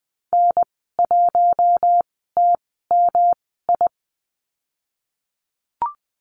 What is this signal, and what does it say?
Morse code, heard as a keyed tone, about 20 wpm, spaced as for 15 wpm, 700 Hz. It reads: N1TMI
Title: 231 Lesson 37g - Letter x - Call Signs 20wpm